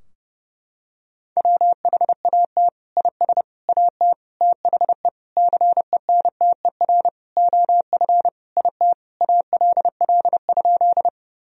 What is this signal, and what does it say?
Morse code, heard as a keyed tone, about 30 wpm, 700 Hz. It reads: WHAT IS AT THE CENTER OF IT ALL?
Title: Sentences from Top 1000 Words - Review Part 2 - 30wpm